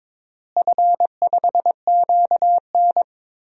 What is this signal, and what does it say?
Morse code, heard as a keyed tone, about 22 wpm, 700 Hz. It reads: F5QN
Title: DXCC-100 Call Signs - 4 Characters - Repeated 3x - 22wpm